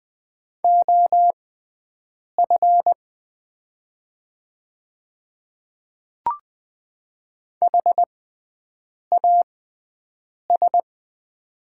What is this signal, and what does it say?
Morse code, heard as a keyed tone, about 20 wpm, spaced as for 7 wpm, 700 Hz. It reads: OF HAS
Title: 6x Character Spacing - Top 100 Words - 20wpm